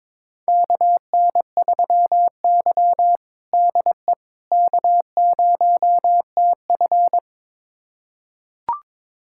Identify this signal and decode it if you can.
Morse code, heard as a keyed tone, about 22 wpm, 700 Hz. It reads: KN3Y DE K0TF
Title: USA Callsign DE Callsign - 4 Characters - 22wpm